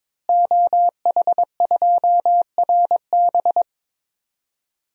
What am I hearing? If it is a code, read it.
Morse code, heard as a keyed tone, about 22 wpm, 700 Hz. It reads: OH2RB
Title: DXCC-100 Call Signs - 5 Characters - Repeated 2x - 22wpm